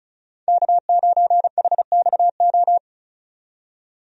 Morse code, heard as a keyed tone, about 36 wpm, 700 Hz. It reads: K9HXO
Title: USA Call Signs - 5 Characters - 35wpm